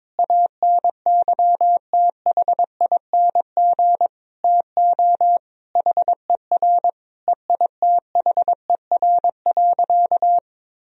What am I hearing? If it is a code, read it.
Morse code, heard as a keyed tone, about 22 wpm, 700 Hz. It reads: ANYTHING TO HER EITHER.
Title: Sentences from Top 1000 Words - Review Part 4 - 22wpm